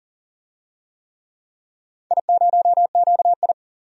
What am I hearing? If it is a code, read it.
Morse code, heard as a keyed tone, about 40 wpm, 700 Hz. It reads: I0QI
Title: DXCC-100 Call Signs - 4 Characters - 40wpm